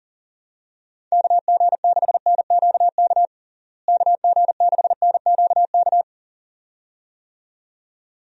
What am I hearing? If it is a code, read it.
Morse code, heard as a keyed tone, about 40 wpm, 700 Hz. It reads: KG6NQK KG6NQK
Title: USA Call Signs - 6 Characters - Repeated 2x - 40wpm